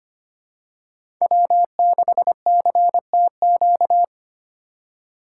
Morse code, heard as a keyed tone, about 25 wpm, 700 Hz. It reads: W6CTQ
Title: USA Call Signs - 5 Characters - 3x Character Spacing - 25wpm